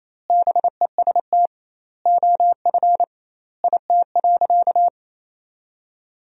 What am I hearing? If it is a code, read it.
Morse code, heard as a keyed tone, about 28 wpm, 700 Hz. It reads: BEST OF IT.
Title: 2x Word-Spacing - Sentences from Top 400 Words - 28wpm